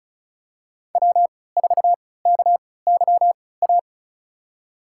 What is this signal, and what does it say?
Morse code, heard as a keyed tone, about 34 wpm, spaced as for 20 wpm, 690 Hz. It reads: W4KYA
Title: USA Call Signs - 5 Characters - 3x Character Spacing - 35wpm